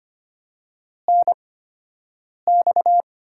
Morse code, heard as a keyed tone, about 25 wpm, spaced as for 7 wpm, 700 Hz. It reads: NX
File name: Top 90 QSO Elements - 8x Character Spacing - 25wpm